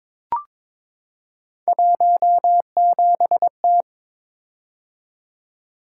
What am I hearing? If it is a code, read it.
Morse code, heard as a keyed tone, about 22 wpm, 700 Hz. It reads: 17T
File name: practice / 22wpm / Three Letter-Number Combinations - 22wpm